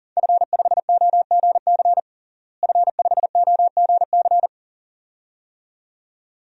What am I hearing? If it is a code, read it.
Morse code, heard as a keyed tone, about 37 wpm, 700 Hz. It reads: F5OGC F5OGC
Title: DXCC-100 Call Signs - 5 Characters - Repeated 2x - 40wpm